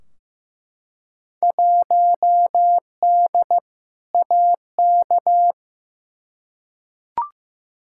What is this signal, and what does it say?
Morse code, heard as a keyed tone, about 15 wpm, 700 Hz. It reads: 1D AK
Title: Field Day Exchanges - 15wpm